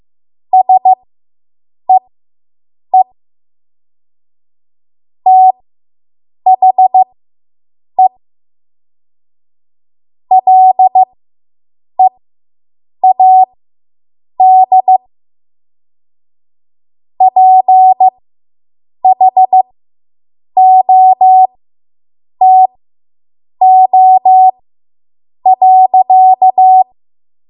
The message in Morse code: SEE THE LEAD PHOTO.